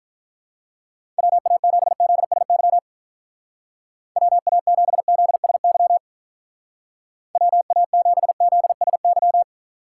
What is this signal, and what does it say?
Morse code, heard as a keyed tone, about 44 wpm, 700 Hz. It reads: WA7ZSY WA7ZSY WA7ZSY